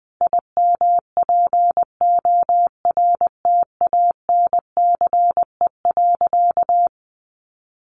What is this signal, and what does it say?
Morse code, heard as a keyed tone, about 20 wpm, 700 Hz. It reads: IMPORTANCE.
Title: Sentences from Top 1200 Words 20wpm